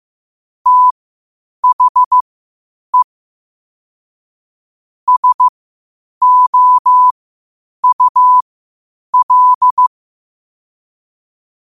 Morse code, heard as a keyed tone, about 15 wpm, spaced as for 8 wpm, 1000 Hz. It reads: THE SOUL
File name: qotdfiles / 8wpm